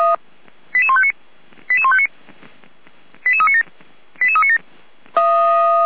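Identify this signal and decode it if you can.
Five-tone selective call (ZVEI), heard as quick runs of five tones, tones of about 70 ms. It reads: 88159 88159 88397 88397
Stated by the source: ZVEI five-tone call received with an SDR.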